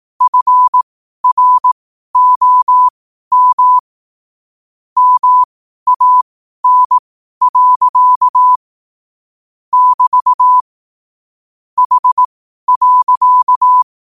Morse code, heard as a keyed tone, about 18 wpm, spaced as for 12 wpm, 1000 Hz. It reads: FROM MAN. <BT> H.
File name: qotdfiles / 12wpm